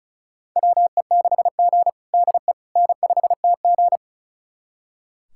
Morse code, heard as a keyed tone, about 36 wpm, 700 Hz. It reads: WE6G DE N5TG
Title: USA Callsign DE Callsign - 4 Characters - 35wpm